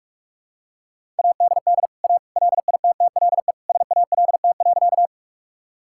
Morse code, heard as a keyed tone, about 43 wpm, 700 Hz. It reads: ADD A LITTLE SALT.
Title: Sentences from Top 1000 Words - Review Part 5 - 45wpm